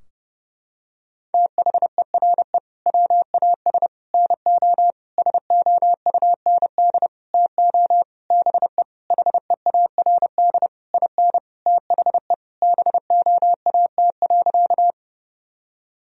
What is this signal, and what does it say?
Morse code, heard as a keyed tone, about 30 wpm, 700 Hz. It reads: THERE WAS NO SOUND TO BE HEARD IN THE BOAT.
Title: Sentences from Top 1000 Words - Review Part 3 - 30wpm